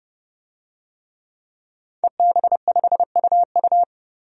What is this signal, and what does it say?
Morse code, heard as a keyed tone, about 29 wpm, 700 Hz. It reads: EB5UU
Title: DXCC-100 Call Signs - 5 Characters - 30wpm